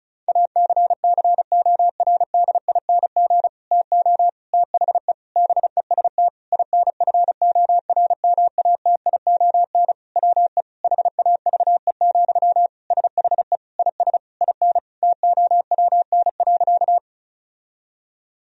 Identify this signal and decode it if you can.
Morse code, heard as a keyed tone, about 34 wpm, 700 Hz. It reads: ACCORDING TO THE BEST INFORMATION WE HAVE, SHE IS IN TOWN.
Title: Sentences from Top 1000 Words - Review Part 4 - 35wpm